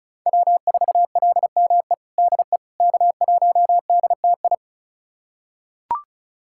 Morse code, heard as a keyed tone, about 34 wpm, 690 Hz. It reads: W4LME DE K1DTI
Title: USA Callsign DE Callsign - 5 Characters - 35wpm